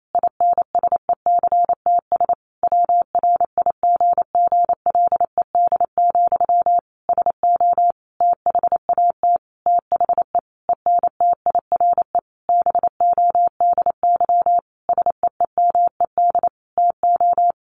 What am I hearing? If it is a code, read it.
Morse code, heard as a keyed tone, about 28 wpm, 700 Hz. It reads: INSECTS WRIGGLED, SO THAT THE ENTIRE BODY SEEMED TO
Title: A Sound Of Thunder 28wpm